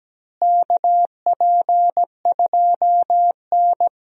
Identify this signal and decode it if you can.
Morse code, heard as a keyed tone, about 17 wpm, 700 Hz. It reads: KP2N
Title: DXCC-100 Call Signs - 4 Characters - Repeated 3x - 17wpm